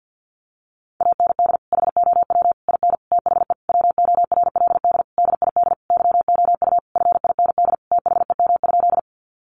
Morse code, heard as a keyed tone, about 50 wpm, 700 Hz. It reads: AND HOW IN THE WORLD DID YOU FIND THEM?
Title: Sentences from Top 200 Words 50wpm